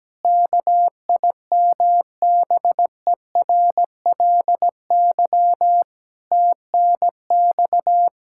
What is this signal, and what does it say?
Morse code, heard as a keyed tone, about 17 wpm, 700 Hz. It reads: KIMBERLY TNX
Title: Practice QSO - Exchange 3 - More Info - 17wpm